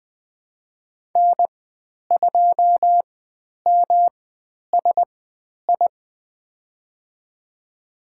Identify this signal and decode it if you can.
Morse code, heard as a keyed tone, about 20 wpm, spaced as for 10 wpm, 700 Hz. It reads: N2MSI
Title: 109 Lesson 18g - Letter m - Call Signs 20wpm